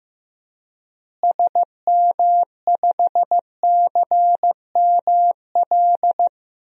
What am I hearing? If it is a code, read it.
Morse code, heard as a keyed tone, about 16 wpm, 700 Hz. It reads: SM5CML
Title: DXCC-100 Call Signs - 6 Characters - Repeated 3x - 15wpm